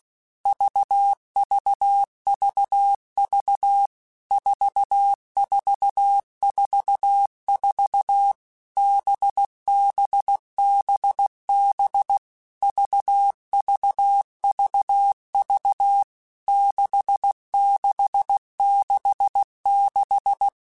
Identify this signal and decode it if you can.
Morse code, heard as a keyed tone, about 16 wpm, 800 Hz. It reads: VVVV 4444 BBBB VVVV 6666
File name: Basic Letter Groups By7Groups / VB46 / VB46@15